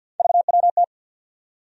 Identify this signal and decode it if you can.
Morse code, heard as a keyed tone, about 52 wpm, 700 Hz. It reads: VWT
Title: USA Call Sign Postfixes - 3 Letters 50wpm